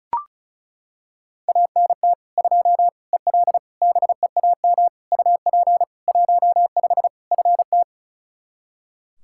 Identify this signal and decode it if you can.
Morse code, heard as a keyed tone, about 34 wpm, 700 Hz. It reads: ANT 2 EL BEAM UP 15 FT
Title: Practice QSO - Exchange 2 - Info - 35wpm